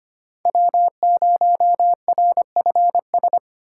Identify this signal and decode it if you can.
Morse code, heard as a keyed tone, about 25 wpm, 700 Hz. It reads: W0RFS
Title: USA Call Signs - 5 Characters - 2x Character Spacing - 25wpm